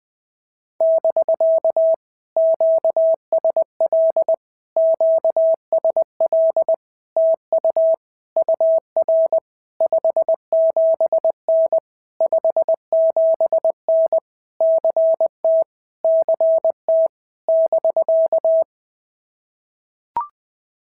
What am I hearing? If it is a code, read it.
Morse code, heard as a keyed tone, about 20 wpm, 650 Hz. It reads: <BK> QSL QSL TU UR 57N 57N CT CT <BK>
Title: POTA QSO Practice - 20wpm